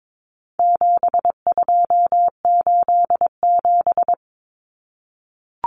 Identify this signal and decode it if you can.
Morse code, heard as a keyed tone, about 22 wpm, 700 Hz. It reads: 7287
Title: Common Cut Numbers - 4-Digit 22wpm